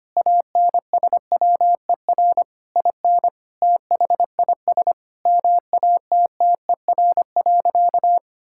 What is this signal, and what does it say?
Morse code, heard as a keyed tone, about 25 wpm, 700 Hz. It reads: ANSWER IN THIS MATTER.
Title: Sentences from Top 1600 Words 25wpm